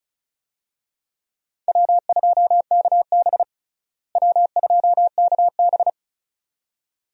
Morse code, heard as a keyed tone, about 36 wpm, 700 Hz. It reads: W2KB W2KB
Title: USA Call Signs - 4 Characters - Repeated 2x - 35wpm